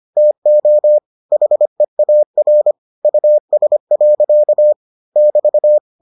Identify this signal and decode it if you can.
Morse code, heard as a keyed tone, about 25 wpm, 600 Hz. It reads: TO HEAR US. <BT>